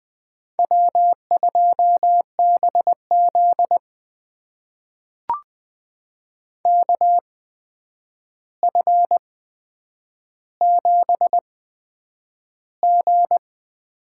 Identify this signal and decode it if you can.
Morse code, heard as a keyed tone, about 20 wpm, spaced as for 9 wpm, 700 Hz. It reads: W2BZ KF7G
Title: USA Call Signs - 8x Character Spacing - 20wpm